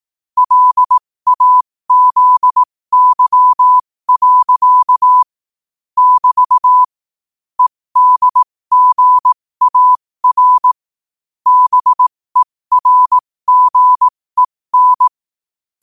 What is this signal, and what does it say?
Morse code, heard as a keyed tone, about 18 wpm, spaced as for 15 wpm, 1000 Hz. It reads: LAZY. <BT> EDGAR BERGEN
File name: qotdfiles / 15wpm